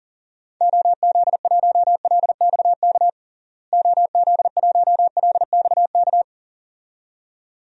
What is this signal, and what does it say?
Morse code, heard as a keyed tone, about 40 wpm, 700 Hz. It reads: OZ1LXK OZ1LXK
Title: DXCC-100 Call Signs - 6 Characters - Repeated 2x - 40wpm